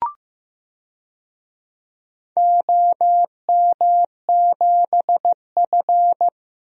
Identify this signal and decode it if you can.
Morse code, heard as a keyed tone, about 15 wpm, 700 Hz. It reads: OM7F
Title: DXCC-100 Call Signs - 4 Characters - 15wpm